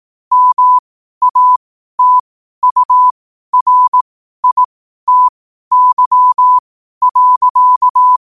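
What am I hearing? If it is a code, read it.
Morse code, heard as a keyed tone, about 18 wpm, spaced as for 12 wpm, 1000 Hz. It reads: MATURITY.
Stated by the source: Morse code quotes at 12 words per minute.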